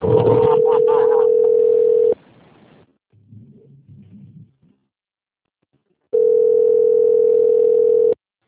Ringback tone (North America), 440 + 480 Hz.
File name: rg-1600-unknown- - - .21309